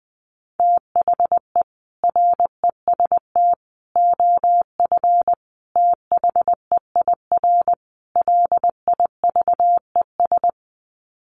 Morse code, heard as a keyed tone, about 20 wpm, 700 Hz. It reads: THE REST OF THEIR LIVES
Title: Sets of 5 Words Encore 20wpm